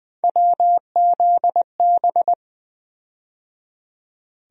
Morse code, heard as a keyed tone, about 20 wpm, 700 Hz. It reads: WZB
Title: USA Call Sign Postfixes - 3 Letters 20wpm